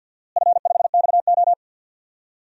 Morse code, heard as a keyed tone, about 52 wpm, 700 Hz. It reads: F5XO
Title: DXCC-100 Call Signs - 4 Characters - Repeated 2x - 50wpm